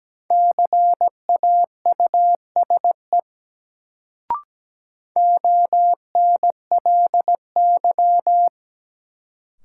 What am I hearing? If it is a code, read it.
Morse code, heard as a keyed tone, about 17 wpm, 700 Hz. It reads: CAUSE ONLY
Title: Sentences from Top 200 Words - New Word Drill - 17wpm